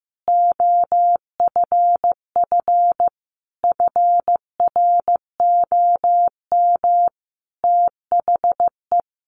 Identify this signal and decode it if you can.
Morse code, heard as a keyed tone, about 15 wpm, 700 Hz. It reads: OFF FROM THE
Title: Sentences from Top 1100 Words 15wpm